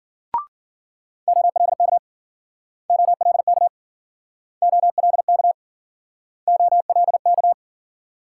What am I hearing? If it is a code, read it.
Morse code, heard as a keyed tone, about 52 wpm, 700 Hz. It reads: OLK OLK OLK OLK